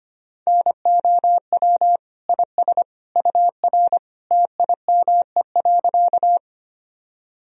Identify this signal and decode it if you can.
Morse code, heard as a keyed tone, about 25 wpm, 700 Hz. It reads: NOW IS UR TIME.
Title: Sentences from Top 100 Words and Abbreviations 25wpm